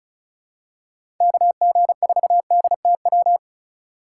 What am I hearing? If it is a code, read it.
Morse code, heard as a keyed tone, about 34 wpm, 690 Hz. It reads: KG4DTW
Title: USA Call Signs - 3x Character Spacing - 35wpm